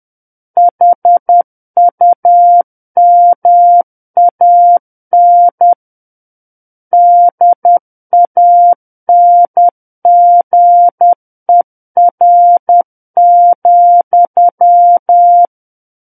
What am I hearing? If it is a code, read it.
Morse code, heard as a keyed tone, about 10 wpm, 700 Hz. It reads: HUMAN DANGER,
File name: Book One / 01-Chapter 1 - 10 WPM